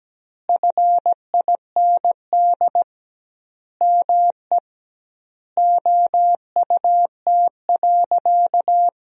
Morse code, heard as a keyed tone, about 17 wpm, 700 Hz. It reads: FIND ME OUT.
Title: 2x Word-Spacing - Sentences from Top 500 Words - 17wpm